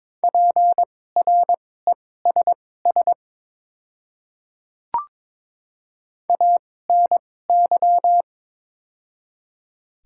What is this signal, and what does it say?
Morse code, heard as a keyed tone, about 22 wpm, spaced as for 16 wpm, 700 Hz. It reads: PRESS ANY
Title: 2x Character Spacing - Top 500 Words - 22wpm